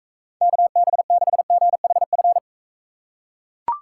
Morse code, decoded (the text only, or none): KB6GHF